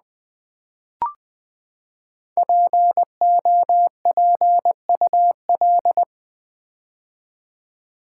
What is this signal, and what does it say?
Morse code, heard as a keyed tone, about 20 wpm, 700 Hz. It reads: POPUL
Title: Build A Word - 5 Letters At A Time - 10-Letter Words - 20wpm